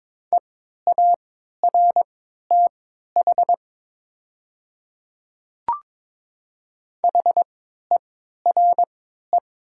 Morse code, heard as a keyed tone, about 22 wpm, spaced as for 12 wpm, 700 Hz. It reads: EARTH HERE
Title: 3x Character Spacing - Top 200 Words - 22wpm